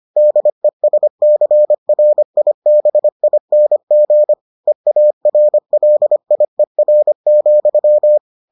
Morse code, heard as a keyed tone, about 25 wpm, 600 Hz. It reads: DESCRIBING EARLIER,